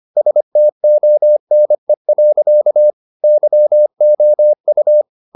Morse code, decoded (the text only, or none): STONE. YOU